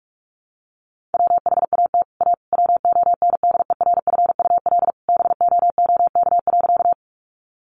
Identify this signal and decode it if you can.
Morse code, heard as a keyed tone, about 44 wpm, 700 Hz. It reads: WHAT A WONDERFUL BOOK.
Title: Sentences from Top 1100 Words 45wpm